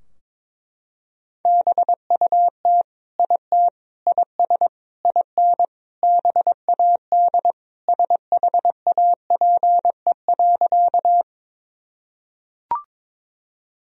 Morse code, heard as a keyed tone, about 21 wpm, 700 Hz. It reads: BUT IT IS IN BAD SHAPE.
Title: Sentences from Top 1000 Words - Review Part 3 - 22wpm